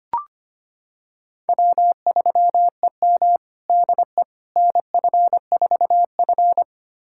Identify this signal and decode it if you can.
Morse code, heard as a keyed tone, about 25 wpm, 700 Hz. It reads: W3EM DE NF4F
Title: USA Callsign DE Callsign - 4 Characters - 25wpm